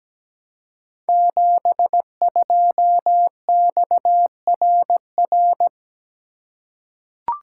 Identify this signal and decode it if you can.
Morse code, heard as a keyed tone, about 17 wpm, 700 Hz. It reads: 72XRR
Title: Code Groups - 5 Letter:Numbers - 17wpm